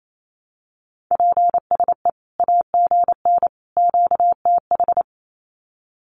Morse code, heard as a keyed tone, about 28 wpm, 700 Hz. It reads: PSE AGN QTH
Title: Common QSO Phrases 28wpm